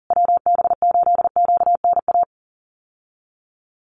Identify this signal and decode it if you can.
Morse code, heard as a keyed tone, about 40 wpm, 700 Hz. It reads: WB8QNA
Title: USA Call Signs Repeated 3x 40wpm